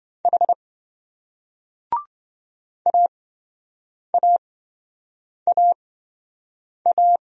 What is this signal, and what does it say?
Morse code, heard as a keyed tone, about 26 wpm, spaced as for 7 wpm, 700 Hz. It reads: H AAAA